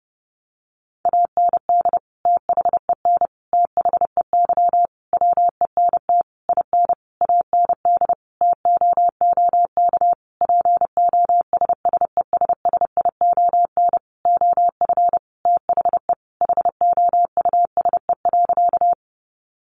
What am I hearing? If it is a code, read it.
Morse code, heard as a keyed tone, about 30 wpm, 700 Hz. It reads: AND THEN THEY WENT IN AND TOOK POSSESSION OF THE HOUSE.
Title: Sentences from Top 1400 Words 30wpm